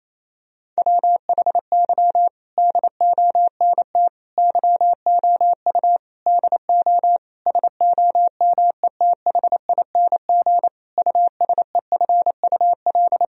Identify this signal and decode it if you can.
Morse code, heard as a keyed tone, about 27 wpm, 700 Hz. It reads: WHY DONT YOU DO SOMETHING USEFUL
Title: Sentences from Top 1000 Words - Review Part 4 - 28wpm